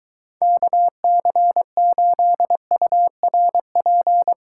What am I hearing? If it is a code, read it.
Morse code, heard as a keyed tone, about 23 wpm, 700 Hz. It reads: KC8URP